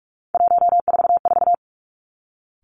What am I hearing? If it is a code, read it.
Morse code, heard as a keyed tone, about 46 wpm, 700 Hz. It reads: 144
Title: Three Challenging Letter-Number Combinations 45wpm